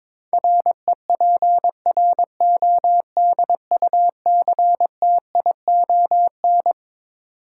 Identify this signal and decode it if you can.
Morse code, heard as a keyed tone, about 22 wpm, 700 Hz. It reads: REPRODUCTION
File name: Twelve Letter Words - 22wpm